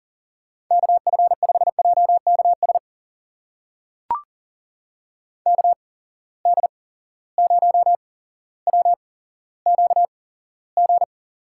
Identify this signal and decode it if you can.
Morse code, heard as a keyed tone, about 42 wpm, spaced as for 17 wpm, 700 Hz. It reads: KF5JKS KD0WQG
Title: USA Call Signs - 6 Characters - 8x Character Spacing - 40wpm